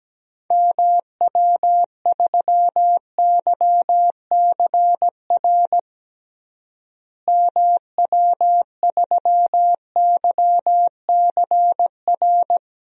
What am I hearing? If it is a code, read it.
Morse code, heard as a keyed tone, about 17 wpm, 700 Hz. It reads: MW3YCR MW3YCR
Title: DXCC-100 Call Signs - 6 Characters - Repeated 3x - 17wpm